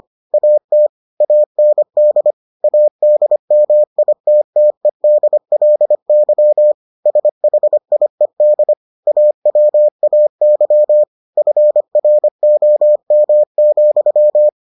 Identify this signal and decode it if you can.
Morse code, heard as a keyed tone, about 25 wpm, 600 Hz. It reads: AT AND ADMITTEDLY SHIED AWAY FROM,